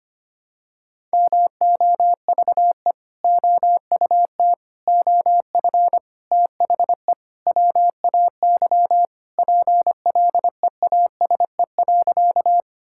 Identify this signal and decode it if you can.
Morse code, heard as a keyed tone, about 25 wpm, 700 Hz. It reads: MOVE OUT OF THE WAY PLEASE.
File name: practice / 25wpm / Sentences with 4-Word Phrases - 25wpm